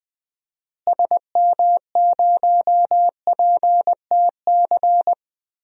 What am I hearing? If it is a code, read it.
Morse code, heard as a keyed tone, about 21 wpm, 700 Hz. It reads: SM0PTC
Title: DXCC-100 Call Signs - 6 Characters - Repeated 3x - 20wpm